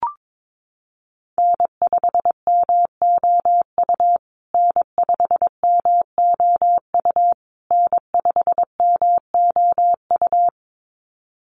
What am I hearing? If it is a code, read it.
Morse code, heard as a keyed tone, about 22 wpm, 700 Hz. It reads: N5MOU N5MOU N5MOU
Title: USA Call Signs Repeated 3x 22wpm